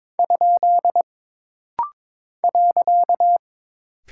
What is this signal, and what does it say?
Morse code, heard as a keyed tone, about 22 wpm, 700 Hz. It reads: ? .
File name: practice / 22wpm / Single Punctuation-Prosign - Rapid-Fire - 22wpm